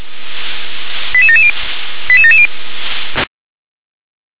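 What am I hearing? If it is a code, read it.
Five-tone selective call (ZVEI), heard as quick runs of five tones, tones of about 70 ms. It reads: 88770 88770
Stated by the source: ZVEI five-tone selective call received on an SDR.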